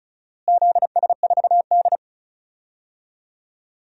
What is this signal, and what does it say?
Morse code, heard as a keyed tone, about 36 wpm, 700 Hz. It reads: ZS4D
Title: DXCC-100 Call Signs - 4 Characters - Repeated 3x - 35wpm